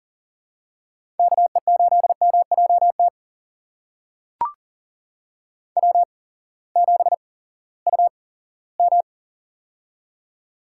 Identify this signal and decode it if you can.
Morse code, heard as a keyed tone, about 40 wpm, spaced as for 20 wpm, 700 Hz. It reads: KE8MJT W7UM
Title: USA Call Signs - 8x Character Spacing - 40wpm